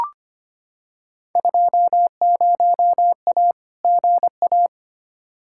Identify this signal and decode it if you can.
Morse code, heard as a keyed tone, about 25 wpm, 700 Hz. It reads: 20A GA
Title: Field Day Exchanges Encore - 25wpm